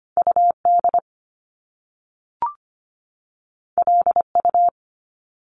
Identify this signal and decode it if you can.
Morse code, heard as a keyed tone, about 25 wpm, 700 Hz. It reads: UD LU
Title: Top Two Letter Combinations in Words 25wpm